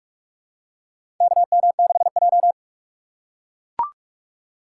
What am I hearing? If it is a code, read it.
Morse code, heard as a keyed tone, about 44 wpm, 700 Hz. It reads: KM6J
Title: USA Call Signs - 4 Characters - 3x Character Spacing - 45wpm